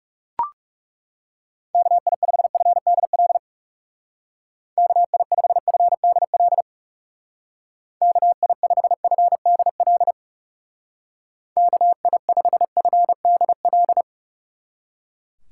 Morse code, heard as a keyed tone, about 37 wpm, 700 Hz. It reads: KI5FDL KI5FDL KI5FDL KI5FDL
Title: Speed Racing - USA 2x3 Call Signs 30wpm